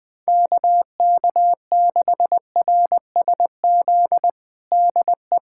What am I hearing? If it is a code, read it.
Morse code, heard as a keyed tone, about 20 wpm, 700 Hz. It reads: KK6RSZ DE
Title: USA Callsign DE Callsign 20wpm